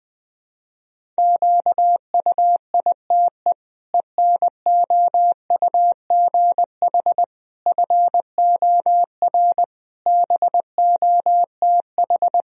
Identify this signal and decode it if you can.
Morse code, heard as a keyed tone, about 20 wpm, 700 Hz. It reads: QUITE ENOUGH FOR BOTH
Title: Sentences from Top 1000 Words - Review Part 1 - 20wpm